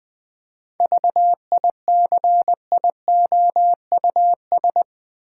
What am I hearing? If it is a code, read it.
Morse code, heard as a keyed tone, about 21 wpm, 700 Hz. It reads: VICIOUS
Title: 7 Letter Words - First 3 Letters Given - 20wpm